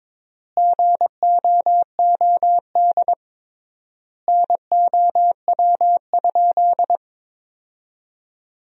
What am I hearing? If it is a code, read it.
Morse code, heard as a keyed tone, about 22 wpm, 700 Hz. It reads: GOOD NOW?
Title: 3x Word-Spacing - Sentences from Top 200 Words - 22wpm